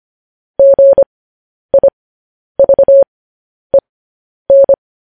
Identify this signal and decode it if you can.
Morse code, heard as a keyed tone, about 25 wpm, spaced as for 10 wpm, 550 Hz. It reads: GIVEN
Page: news-headlines-in-morse-code-at-10-wpm